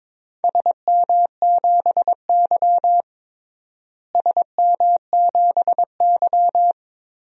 Morse code, heard as a keyed tone, about 21 wpm, 700 Hz. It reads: SM7Y SM7Y
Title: DXCC-100 Call Signs - 4 Characters - Repeated 3x - 22wpm